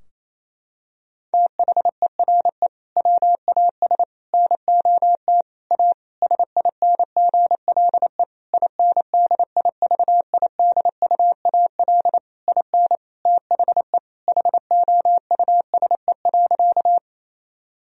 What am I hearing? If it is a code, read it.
Morse code, heard as a keyed tone, about 28 wpm, 700 Hz. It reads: THERE WAS NOT A SINGLE INDIVIDUAL IN THE HOUSE.
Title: Sentences from Top 1000 Words - Review Part 4 - 28wpm